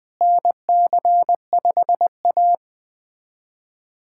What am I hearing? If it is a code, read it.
Morse code, heard as a keyed tone, about 20 wpm, 700 Hz. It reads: NC5A
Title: USA Callsign DE Callsign - 4 Characters - 20wpm